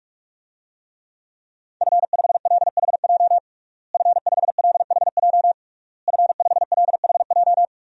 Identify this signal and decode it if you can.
Morse code, heard as a keyed tone, about 46 wpm, 700 Hz. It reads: F5LHJ F5LHJ F5LHJ
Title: DXCC-100 Call Signs - 5 Characters - Repeated 3x - 45wpm